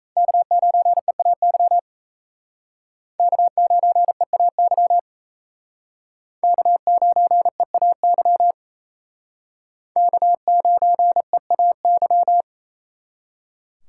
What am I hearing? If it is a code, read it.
Morse code, heard as a keyed tone, about 35 wpm, 700 Hz. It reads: K9EAY K9EAY K9EAY K9EAY
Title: Speed Racing - USA 1x3 Call Signs 28wpm